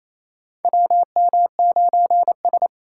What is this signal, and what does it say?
Morse code, heard as a keyed tone, about 28 wpm, 700 Hz. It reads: WM9S
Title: USA Call Signs - 2x Character Spacing - 28wpm